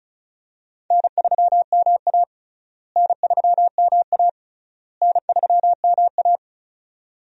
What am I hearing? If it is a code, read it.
Morse code, heard as a keyed tone, about 34 wpm, 700 Hz. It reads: N3MA N3MA N3MA
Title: USA Call Signs - 4 Characters - Repeated 3x - 35wpm